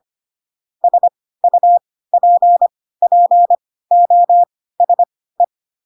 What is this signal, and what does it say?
Morse code, heard as a keyed tone, about 25 wpm, spaced as for 16 wpm, 700 Hz. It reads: SUPPOSE
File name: Pinocchio / Pinocchio ch2 10 25 ik1hge